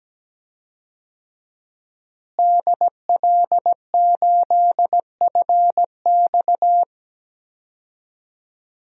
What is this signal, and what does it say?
Morse code, heard as a keyed tone, about 17 wpm, 700 Hz. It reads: DL8FX
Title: DXCC-100 Call Signs - 5 Characters - Repeated 3x - 17wpm